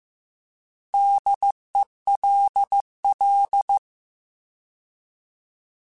Morse code, heard as a keyed tone, about 15 wpm, 800 Hz. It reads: DELL
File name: Alphabetical / Alphabetical mp3s / Ds @13